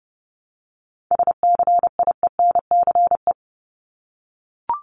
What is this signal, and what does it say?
Morse code, heard as a keyed tone, about 30 wpm, 700 Hz. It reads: SCIENCE
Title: Top 500 Words 30wpm